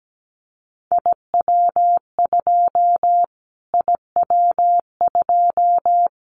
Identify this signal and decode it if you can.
Morse code, heard as a keyed tone, about 17 wpm, 700 Hz. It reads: IW2 IW2
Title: Callsign Prefixes for Top 25 DXCC - Repeated 3x - 17wpm